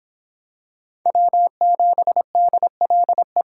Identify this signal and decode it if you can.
Morse code, heard as a keyed tone, about 25 wpm, 700 Hz. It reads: W7DLE